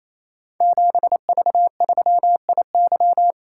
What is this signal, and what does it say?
Morse code, heard as a keyed tone, about 29 wpm, 700 Hz. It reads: 7V3IY
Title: Code Groups - 5 Letter:Numbers - 28wpm